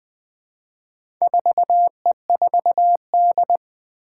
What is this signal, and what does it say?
Morse code, heard as a keyed tone, about 20 wpm, 700 Hz. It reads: 4E4D
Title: Four Challenging Letter-Number Combinations 20wpm